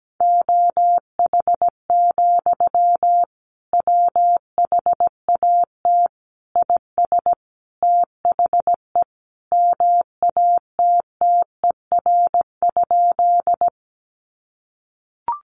Morse code, heard as a keyed tone, about 17 wpm, 700 Hz. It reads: OH, WHAT IS THE MATTER?
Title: Sentences from Top 700 Words 17wpm